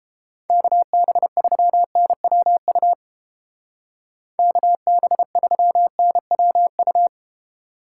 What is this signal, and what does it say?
Morse code, heard as a keyed tone, about 31 wpm, 700 Hz. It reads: KB3NWU KB3NWU